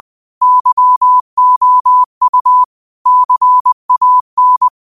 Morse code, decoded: YOU CAN

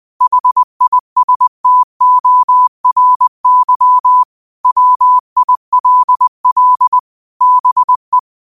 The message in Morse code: HISTORY WILL BE

